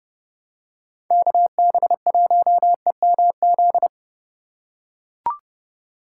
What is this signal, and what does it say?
Morse code, heard as a keyed tone, about 30 wpm, 700 Hz. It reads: KB1EMZ
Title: USA Call Signs - 6 Characters - 3x Character Spacing - 30wpm